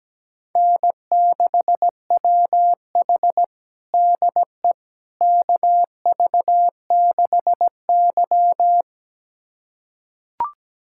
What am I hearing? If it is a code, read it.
Morse code, heard as a keyed tone, about 17 wpm, 700 Hz. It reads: N6WH DE KV6Y
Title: USA Callsign DE Callsign - 4 Characters - 17wpm